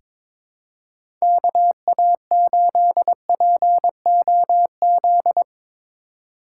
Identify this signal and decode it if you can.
Morse code, heard as a keyed tone, about 22 wpm, 700 Hz. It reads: KA8POZ
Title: K1USN SST QSO Practice - 22wpm